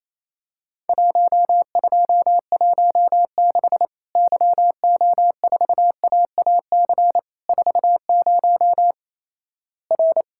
Morse code, heard as a keyed tone, about 28 wpm, 700 Hz. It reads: 1216 YO4AAC 40 R
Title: Worked All Europe DX Contest - QTC Exchange - 28wpm